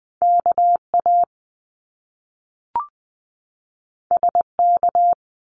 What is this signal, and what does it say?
Morse code, heard as a keyed tone, about 20 wpm, 700 Hz. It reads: KA SK
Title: Top Two Letter Combinations in Words 20wpm